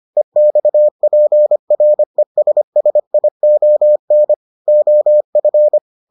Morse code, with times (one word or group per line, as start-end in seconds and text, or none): 0.2-4.3 EXPRESSION
4.7-5.8 OF